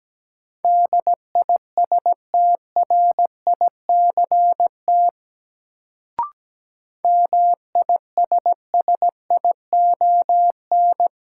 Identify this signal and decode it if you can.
Morse code, heard as a keyed tone, about 17 wpm, 700 Hz. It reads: DISTRICT MISSION
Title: Sentences from Top 1600 Words - New Word Drill Alt - 17wpm